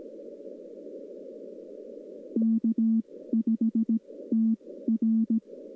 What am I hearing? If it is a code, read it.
Morse code, heard as a keyed tone, about 17 wpm, spaced as for 14 wpm, 220 Hz. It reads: K5TR
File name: contests / 2019-june=vhf-k5tr- -0101 0-2